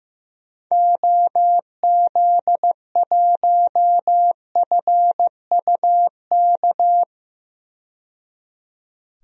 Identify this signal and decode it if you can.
Morse code, heard as a keyed tone, about 15 wpm, 690 Hz. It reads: OZ1FUK
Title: DXCC-100 Call Signs - 6 Characters - 15wpm